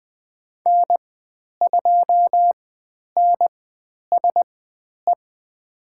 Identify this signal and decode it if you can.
Morse code, heard as a keyed tone, about 20 wpm, spaced as for 10 wpm, 700 Hz. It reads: N2NSE
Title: 095 Lesson 16g - Letter c - Call Signs 20wpm